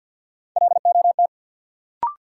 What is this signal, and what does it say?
Morse code, heard as a keyed tone, about 53 wpm, 700 Hz. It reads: LOT
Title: Sentences from Top 500 Words - New Word Drill - 50wpm